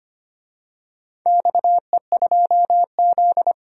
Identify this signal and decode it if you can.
Morse code, heard as a keyed tone, about 25 wpm, 700 Hz. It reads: XE2Z
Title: DXCC-100 Call Signs - 4 Characters - 25wpm